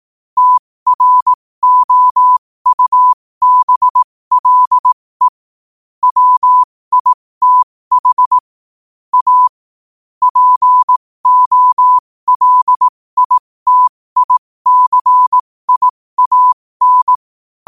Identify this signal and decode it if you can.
Morse code, heard as a keyed tone, about 18 wpm, spaced as for 15 wpm, 1000 Hz. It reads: TROUBLE WITH A POLITICIAN
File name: qotdfiles / 15wpm